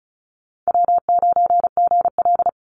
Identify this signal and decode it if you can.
Morse code, heard as a keyed tone, about 34 wpm, 700 Hz. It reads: W9GL
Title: Call Signs - 35wpm